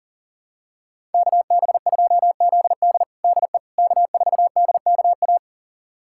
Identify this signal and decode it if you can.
Morse code, heard as a keyed tone, about 40 wpm, 700 Hz. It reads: KB2ZD DE K4DKA
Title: USA Callsign DE Callsign - 5 Characters - 40wpm